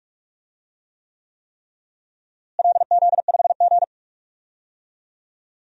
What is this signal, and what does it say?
Morse code, heard as a keyed tone, about 46 wpm, 700 Hz. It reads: LZ5G
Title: DXCC-100 Call Signs - 4 Characters - Repeated 3x - 45wpm